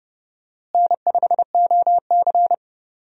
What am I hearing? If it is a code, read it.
Morse code, heard as a keyed tone, about 30 wpm, 700 Hz. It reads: N5OC
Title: USA Call Signs - 4 Characters - Repeated 2x - 30wpm